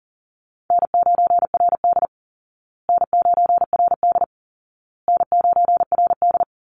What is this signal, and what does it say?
Morse code, heard as a keyed tone, about 40 wpm, 700 Hz. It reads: N9RD N9RD N9RD
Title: Easy Call Signs Repeated 3x - 40wpm